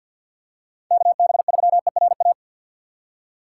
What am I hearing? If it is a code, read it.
Morse code, heard as a keyed tone, about 52 wpm, 700 Hz. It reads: KB3ERA
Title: USA Call Signs - 6x Character Spacing - 50wpm